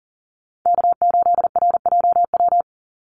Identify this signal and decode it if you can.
Morse code, heard as a keyed tone, about 40 wpm, 700 Hz. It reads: K8RJW
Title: Call Signs - 40wpm